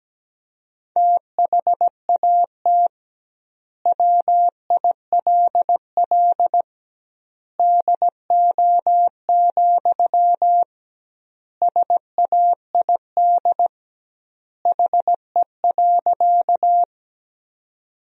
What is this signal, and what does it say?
Morse code, heard as a keyed tone, about 17 wpm, 700 Hz. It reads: THAT WILL DO, SAID HE.
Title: 2x Word-Spacing - Sentences from Top 100 Words - 17wpm